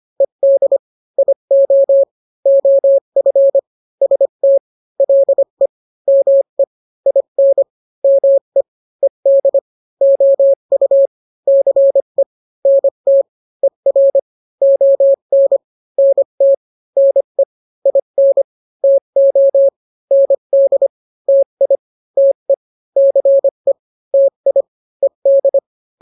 Morse code, heard as a keyed tone, about 25 wpm, 550 Hz. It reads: ED IO OF ST LE ME IN ME ED OU CE NT ER ON NT NE IN TO ND TI TE CE TI ED